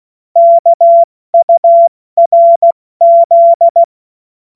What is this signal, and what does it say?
Morse code, heard as a keyed tone, about 16 wpm, spaced as for 14 wpm, 680 Hz. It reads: KURZ
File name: Morsekurs / Witze / Witz 09. August 2022 70 F80